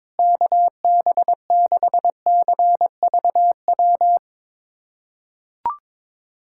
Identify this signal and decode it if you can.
Morse code, heard as a keyed tone, about 22 wpm, 700 Hz. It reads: KB6CVW
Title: USA Callsign DE Callsign 22wpm